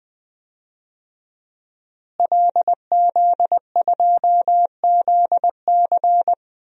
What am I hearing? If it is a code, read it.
Morse code, heard as a keyed tone, about 20 wpm, 700 Hz. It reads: LZ2ZC
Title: DXCC-100 Call Signs - 5 Characters - Repeated 2x - 20wpm